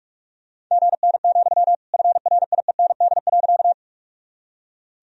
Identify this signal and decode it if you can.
Morse code, heard as a keyed tone, about 42 wpm, 700 Hz. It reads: GN, FRIEND.
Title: Sentences from Top 300 Words and Abbreviations 45wpm